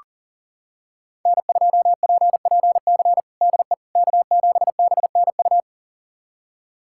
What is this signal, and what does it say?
Morse code, heard as a keyed tone, about 37 wpm, 690 Hz. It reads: N2PPC DE K7BNU
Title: USA Callsign DE Callsign - 5 Characters - 40wpm